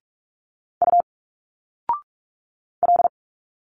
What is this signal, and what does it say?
Morse code, heard as a keyed tone, about 44 wpm, 700 Hz. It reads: U L
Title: Single Letters 45wpm